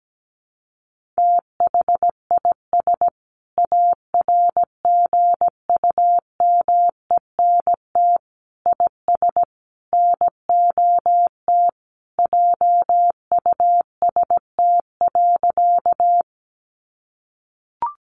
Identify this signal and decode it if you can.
Morse code, heard as a keyed tone, about 17 wpm, 700 Hz. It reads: THIS ARGUMENT IS NOT JUST.
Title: Sentences from Top 1400 Words 17wpm